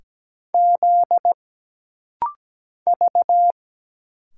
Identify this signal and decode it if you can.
Morse code, heard as a keyed tone, about 17 wpm, 700 Hz. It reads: Z V
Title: Single Letters - Rapid-Fire 17wpm